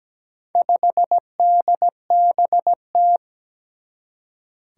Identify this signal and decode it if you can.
Morse code, heard as a keyed tone, about 17 wpm, 700 Hz. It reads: 5DBT
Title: Four Challenging Letter-Number Combinations 17wpm